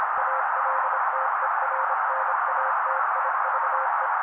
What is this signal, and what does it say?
Morse code, heard as a keyed tone, about 26 wpm, 530 Hz. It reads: ALTERNATIVE
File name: daily / 25wpm noise